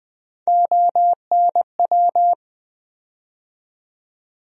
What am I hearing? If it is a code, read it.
Morse code, heard as a keyed tone, about 20 wpm, 700 Hz. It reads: ONW
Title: USA Call Sign Postfixes - 3 Letters 20wpm